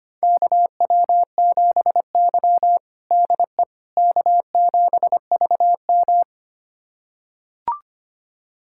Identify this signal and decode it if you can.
Morse code, heard as a keyed tone, about 25 wpm, 700 Hz. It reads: KW7Y DE K7VM
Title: USA Callsign DE Callsign - 4 Characters - 25wpm